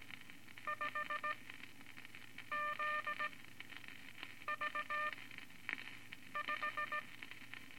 Morse code, heard as a keyed tone, about 17 wpm, spaced as for 6 wpm, 1200 Hz, with noise. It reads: 5ZV5
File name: Morse Code Records / 13A LCM 154 Units I-IV Groups of Four cont